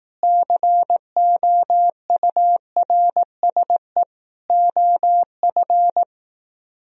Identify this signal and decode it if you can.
Morse code, heard as a keyed tone, about 18 wpm, 700 Hz. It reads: COURSE OF
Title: Speed-Racing - Sets of 4 Words 15wpm